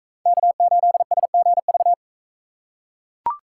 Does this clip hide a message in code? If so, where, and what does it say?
Yes: Morse code from 0.3 to 1.9 s, heard as a keyed tone, about 43 wpm, 700 Hz. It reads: K8SGV